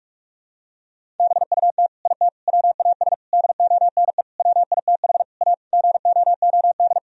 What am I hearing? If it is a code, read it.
Morse code, heard as a keyed tone, about 44 wpm, 700 Hz. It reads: BUT IT WAS DONE WITH A GOOD
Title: Sentences from Top 1000 Words - Review Part 4 - 45wpm